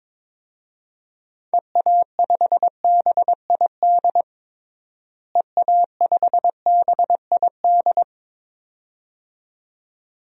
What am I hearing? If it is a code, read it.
Morse code, heard as a keyed tone, about 22 wpm, 700 Hz. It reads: EA5BID EA5BID
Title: DXCC-100 Call Signs - 6 Characters - Repeated 2x - 22wpm